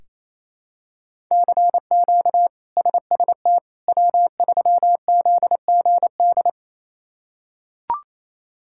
Morse code, heard as a keyed tone, about 27 wpm, 700 Hz. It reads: CQ SST W3ZGD
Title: K1USN SST QSO Practice - 28wpm